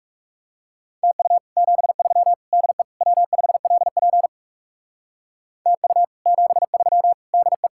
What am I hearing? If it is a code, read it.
Morse code, heard as a keyed tone, about 39 wpm, 700 Hz. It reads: TU 73 DE W5LP TU 73 DE